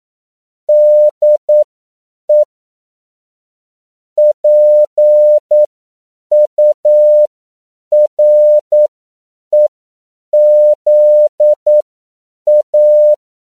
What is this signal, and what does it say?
Morse code, heard as a keyed tone, about 9 wpm, spaced as for 7 wpm, 600 Hz. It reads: DE PUREZA
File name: CW / 10PPM